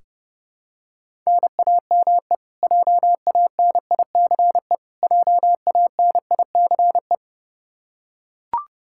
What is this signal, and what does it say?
Morse code, heard as a keyed tone, about 30 wpm, 700 Hz. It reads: NAME JANICE JANICE
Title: Practice QSO - Exchange 1 - Beginning - 30wpm